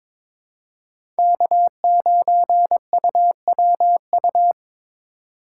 Morse code, heard as a keyed tone, about 22 wpm, 700 Hz. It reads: K9UWU
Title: USA Call Signs - 5 Characters - 3x Character Spacing - 22wpm